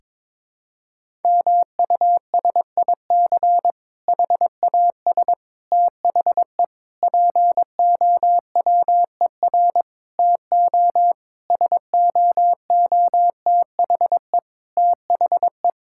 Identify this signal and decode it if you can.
Morse code, heard as a keyed tone, about 22 wpm, 700 Hz. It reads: MUSIC HAS THE POWER TO SOOTHE THE
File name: Difficult Sentences with Hints - 22wpm